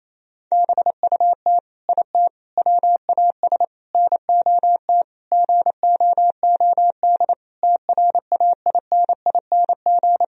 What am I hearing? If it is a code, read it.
Morse code, heard as a keyed tone, about 27 wpm, 700 Hz. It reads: BUT IT WAS NOT GOOD TRAINING
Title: Sentences from Top 1000 Words - Review Part 4 - 28wpm